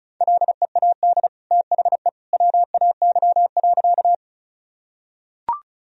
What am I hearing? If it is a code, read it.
Morse code, heard as a keyed tone, about 34 wpm, 700 Hz. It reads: LEAD THE WAY.
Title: Sentences from Top 1000 Words - Review Part 2 - 35wpm